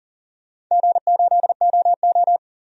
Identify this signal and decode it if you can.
Morse code, heard as a keyed tone, about 37 wpm, 700 Hz. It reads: G8OO
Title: DXCC-100 Call Signs - 4 Characters - 40wpm